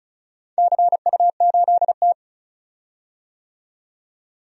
Morse code, heard as a keyed tone, about 34 wpm, 700 Hz. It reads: CU8T
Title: DXCC-100 Call Signs - 4 Characters - 35wpm